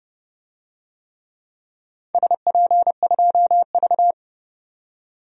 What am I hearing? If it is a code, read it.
Morse code, heard as a keyed tone, about 30 wpm, 700 Hz. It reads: SP2V
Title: DXCC-100 Call Signs - 4 Characters - 30wpm